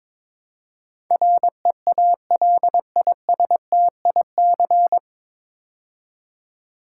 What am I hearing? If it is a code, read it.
Morse code, heard as a keyed tone, about 22 wpm, 700 Hz. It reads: REALISTIC
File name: 9 Letter Words - First 3 Letters Given - 22wpm